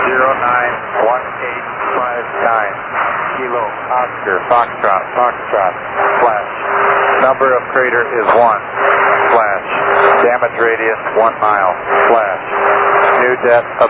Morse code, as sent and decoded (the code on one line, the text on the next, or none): none